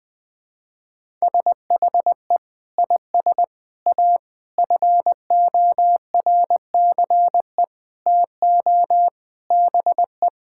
Morse code, heard as a keyed tone, about 20 wpm, 700 Hz. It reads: SHE IS A FORCE TO BE
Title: Sentences with 6-Word Phrases - 20wpm